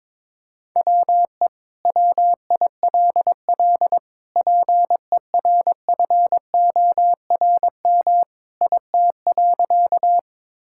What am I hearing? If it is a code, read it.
Morse code, heard as a keyed tone, about 22 wpm, 700 Hz. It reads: WE WILL PERFORM IT.
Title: Sentences from Top 1000 Words - Review Part 5 - 22wpm